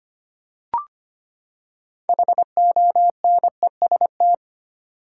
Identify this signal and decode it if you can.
Morse code, heard as a keyed tone, about 25 wpm, 700 Hz. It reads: HONEST
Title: Six Letter Words - 25wpm